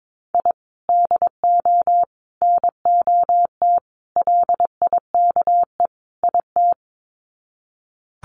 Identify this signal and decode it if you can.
Morse code, heard as a keyed tone, about 22 wpm, 700 Hz. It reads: I DO NOT LIKE IT
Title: Sets of 5 Words Encore 22wpm